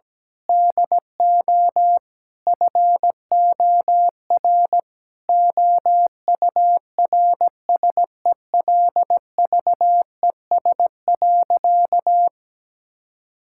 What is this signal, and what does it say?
Morse code, heard as a keyed tone, about 17 wpm, 700 Hz. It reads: DO FOR OURSELVES.
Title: Sentences from Top 1600 Words 17wpm